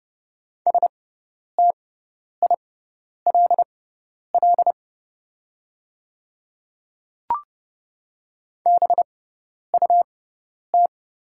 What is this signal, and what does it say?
Morse code, heard as a keyed tone, about 30 wpm, spaced as for 10 wpm, 700 Hz. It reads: STILL BUT
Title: 6x Character Spacing - Top 300 Words - 30wpm